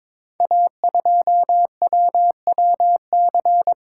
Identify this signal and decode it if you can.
Morse code, heard as a keyed tone, about 22 wpm, 700 Hz. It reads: A2WWC
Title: Code Groups - 5 Letter:Numbers - 22wpm